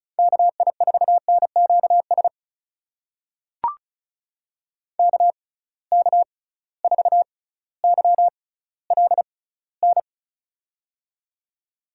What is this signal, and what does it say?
Morse code, heard as a keyed tone, about 36 wpm, spaced as for 18 wpm, 700 Hz. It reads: KI4NQS KK4YLN
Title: USA Call Signs - 6x Character Spacing - 35wpm